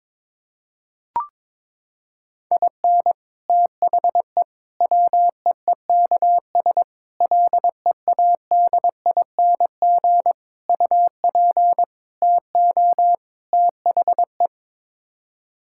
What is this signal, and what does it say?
Morse code, heard as a keyed tone, about 22 wpm, 700 Hz. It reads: IN THE WEEKS LEADING UP TO THE
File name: practice / 22wpm / Sets of 7 Words - 22wpm